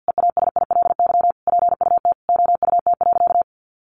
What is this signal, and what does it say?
Morse code, heard as a keyed tone, about 50 wpm, 700 Hz. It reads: EASILY PUT OUT.
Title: Sentences from Top 1400 Words 50wpm